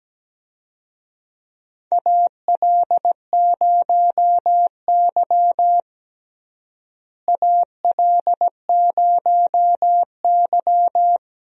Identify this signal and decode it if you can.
Morse code, heard as a keyed tone, about 17 wpm, 700 Hz. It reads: AL0Y AL0Y
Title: USA Call Signs - 4 Characters - Repeated 3x - 17wpm